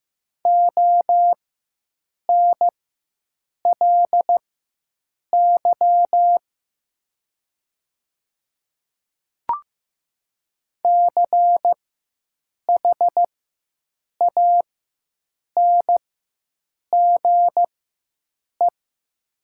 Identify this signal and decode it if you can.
Morse code, heard as a keyed tone, about 15 wpm, spaced as for 7 wpm, 700 Hz. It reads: ONLY CHANGE
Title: 4x Character Spacing - Top 300 Words - 15wpm